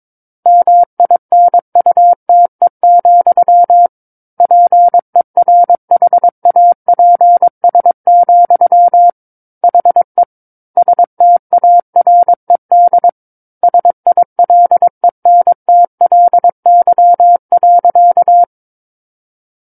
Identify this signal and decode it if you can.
Morse code, heard as a keyed tone, about 22 wpm, 700 Hz. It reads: MINUTE, PERHAPS, HE STARED SILENTLY.
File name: Book One / 13-Chapter 13 - 22 WPM